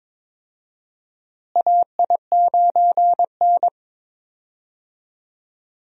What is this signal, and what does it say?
Morse code, heard as a keyed tone, about 22 wpm, 700 Hz. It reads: AI9N
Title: USA Call Signs - 4 Characters - Repeated 3x - 22wpm